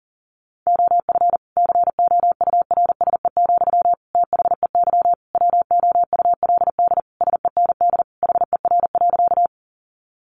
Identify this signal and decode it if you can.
Morse code, heard as a keyed tone, about 40 wpm, 700 Hz. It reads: OF COURSE, THEY WOULD SEND HER.
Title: Sentences from Top 1500 Words 40wpm